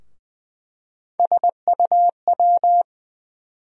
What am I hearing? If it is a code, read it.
Morse code, heard as a keyed tone, about 20 wpm, 700 Hz. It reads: SUW
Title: USA Call Sign Postfixes - 3 Letters 20wpm